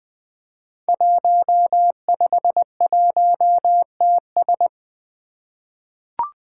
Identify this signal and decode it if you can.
Morse code, heard as a keyed tone, about 20 wpm, 700 Hz. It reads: 151TS
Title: Five Challenging Letter-Number Combinations 20wpm